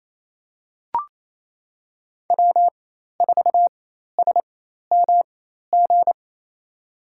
Morse code, heard as a keyed tone, about 28 wpm, spaced as for 13 wpm, 700 Hz. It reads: W4SMG
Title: USA Call Signs - 5 Characters - 4x Character Spacing - 28wpm